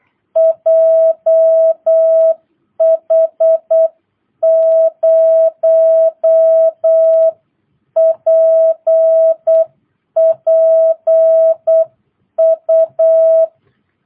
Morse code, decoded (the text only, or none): JH0PPU